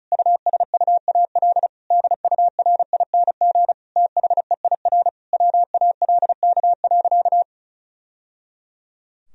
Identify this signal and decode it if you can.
Morse code, heard as a keyed tone, about 34 wpm, 700 Hz. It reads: USUAL DURING THEIR WALK.
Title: Sentences from Top 1000 Words - Review Part 2 - 35wpm